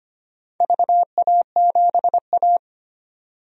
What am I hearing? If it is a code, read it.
Morse code, heard as a keyed tone, about 25 wpm, 700 Hz. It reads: VA7A
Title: DXCC-100 Call Signs - 4 Characters - 25wpm